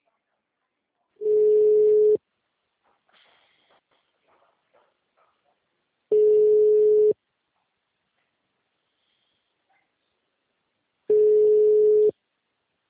Ringback tone (Europe, 425 Hz).
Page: Atibaia - SP 1 (CP Cartao)